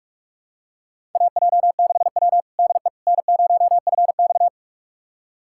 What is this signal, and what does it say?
Morse code, heard as a keyed tone, about 46 wpm, 700 Hz. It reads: AJ6W DE N0FX
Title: USA Callsign DE Callsign - 4 Characters - 45wpm